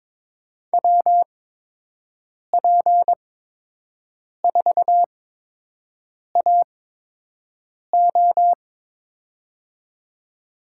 Morse code, heard as a keyed tone, about 22 wpm, spaced as for 6 wpm, 700 Hz. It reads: WP4AO
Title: USA Call Signs - 8x Character Spacing - 22wpm